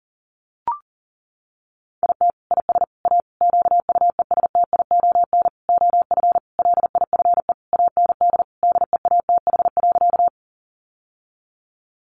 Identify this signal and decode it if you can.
Morse code, heard as a keyed tone, about 40 wpm, 700 Hz. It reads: IT IS A QUESTION OF LIFE AND DEATH.
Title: Sentences from Top 1100 Words 40wpm